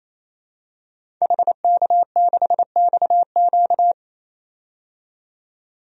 Morse code, heard as a keyed tone, about 28 wpm, 700 Hz. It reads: HK6XQ
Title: DXCC-100 Call Signs - 5 Characters - 28wpm